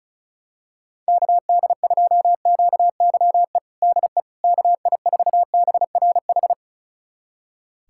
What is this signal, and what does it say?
Morse code, heard as a keyed tone, about 34 wpm, 700 Hz. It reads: KD2QYE DE KI4BRH
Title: USA Callsign DE Callsign 35wpm